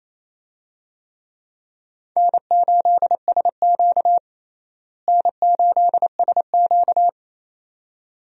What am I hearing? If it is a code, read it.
Morse code, heard as a keyed tone, about 28 wpm, 700 Hz. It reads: N8SQ N8SQ
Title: USA Call Signs - 4 Characters - Repeated 2x - 28wpm